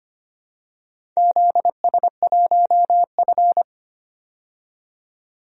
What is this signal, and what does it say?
Morse code, heard as a keyed tone, about 25 wpm, 700 Hz. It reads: ZS1F
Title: DXCC-100 Call Signs - 4 Characters - 25wpm